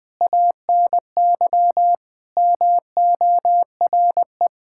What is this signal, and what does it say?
Morse code, heard as a keyed tone, about 20 wpm, 700 Hz. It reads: ANY MORE
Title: Sentences with 5-Word Phrases - 20wpm